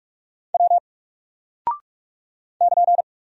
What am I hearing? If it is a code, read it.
Morse code, heard as a keyed tone, about 44 wpm, 700 Hz. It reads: W <KN>
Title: Single Character (Letter:Number:Punctuation:Prosign) 45wpm